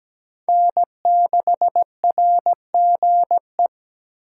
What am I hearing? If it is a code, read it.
Morse code, heard as a keyed tone, about 17 wpm, 700 Hz. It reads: N6RGE
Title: USA Call Signs - 5 Characters - 2x Character Spacing - 17wpm